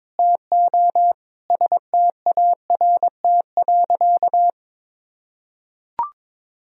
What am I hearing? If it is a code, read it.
Morse code, heard as a keyed tone, about 22 wpm, 700 Hz. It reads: TO START.
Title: Sentences from Top 1000 Words - Review Part 1 - 22wpm